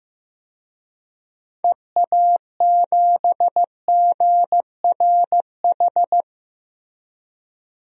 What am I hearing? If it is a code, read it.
Morse code, heard as a keyed tone, about 15 wpm, 700 Hz. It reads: EA7GRH
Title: DXCC-100 Call Signs - 6 Characters - Repeated 2x - 15wpm